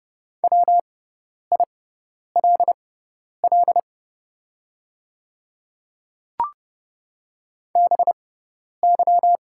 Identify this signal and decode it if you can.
Morse code, heard as a keyed tone, about 31 wpm, spaced as for 10 wpm, 690 Hz. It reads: WILL BY
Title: 6x Character Spacing - Top 200 Words - 30wpm